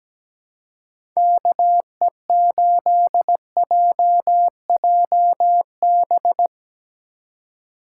Morse code, heard as a keyed tone, about 17 wpm, 700 Hz. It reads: KE8JJB
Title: USA Call Signs - 6 Characters - 3x Character Spacing - 17wpm